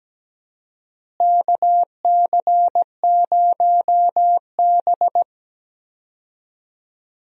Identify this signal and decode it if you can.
Morse code, heard as a keyed tone, about 17 wpm, 700 Hz. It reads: KC0B
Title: USA Call Signs - 4 Characters - Repeated 2x - 17wpm